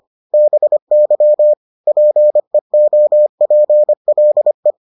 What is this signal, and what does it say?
Morse code, heard as a keyed tone, about 25 wpm, 600 Hz. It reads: BY PEOPLE